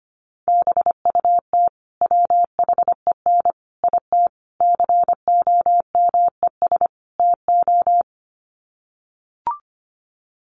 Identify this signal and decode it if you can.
Morse code, heard as a keyed tone, about 25 wpm, 700 Hz. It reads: BUT WHEN IT COMES TO
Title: Sets of 5 Words 25wpm